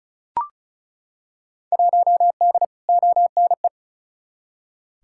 Morse code, heard as a keyed tone, about 34 wpm, 700 Hz. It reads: 1D ONE
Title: Field Day Exchanges - 35wpm